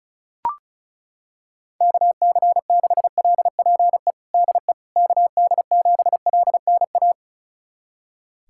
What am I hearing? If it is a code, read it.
Morse code, heard as a keyed tone, about 34 wpm, 700 Hz. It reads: KC6LPE DE KD7LNA
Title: USA Callsign DE Callsign - 6 Characters - 35wpm